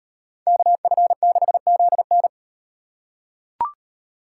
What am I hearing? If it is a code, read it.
Morse code, heard as a keyed tone, about 38 wpm, 700 Hz. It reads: KF6ZN